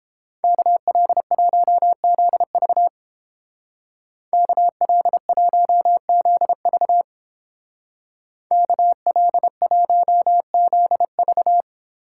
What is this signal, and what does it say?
Morse code, heard as a keyed tone, about 29 wpm, 700 Hz. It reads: KL1ZV KL1ZV KL1ZV